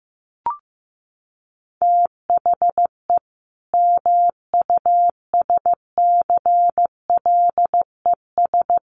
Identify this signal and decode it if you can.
Morse code, heard as a keyed tone, about 15 wpm, 700 Hz. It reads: THE MUSCLES
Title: A Sound Of Thunder 15wpm